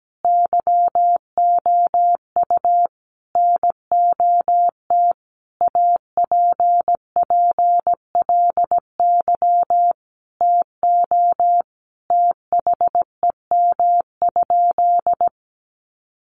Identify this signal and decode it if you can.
Morse code, heard as a keyed tone, about 17 wpm, 700 Hz. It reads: YOU NOT APPLY TO THEM?
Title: Sentences from Top 1200 Words 17wpm